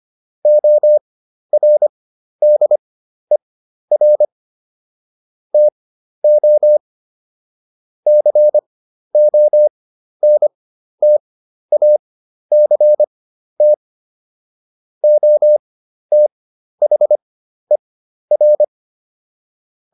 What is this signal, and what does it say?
Morse code, heard as a keyed tone, about 25 wpm, spaced as for 12 wpm, 600 Hz. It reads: ORDER TO CONTACT OTHER